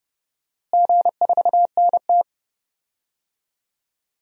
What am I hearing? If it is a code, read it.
Morse code, heard as a keyed tone, about 30 wpm, 700 Hz. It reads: G4NT
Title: DXCC-100 Call Signs - 4 Characters - Repeated 2x - 30wpm